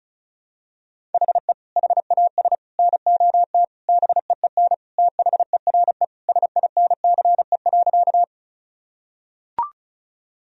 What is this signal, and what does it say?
Morse code, heard as a keyed tone, about 34 wpm, 700 Hz. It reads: HE HAS NOT BEEN THERE SINCE.
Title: Sentences from Top 1000 Words - Review Part 1 - 35wpm